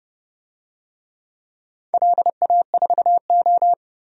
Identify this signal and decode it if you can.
Morse code, heard as a keyed tone, about 30 wpm, 700 Hz. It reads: LA4O
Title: DXCC-100 Call Signs - 4 Characters - 30wpm